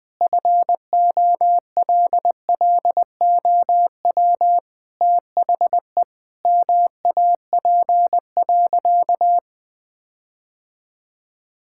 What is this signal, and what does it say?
Morse code, heard as a keyed tone, about 20 wpm, 700 Hz. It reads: FOLLOW THE MAP.
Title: Sentences with 8-Word Phrases - 20wpm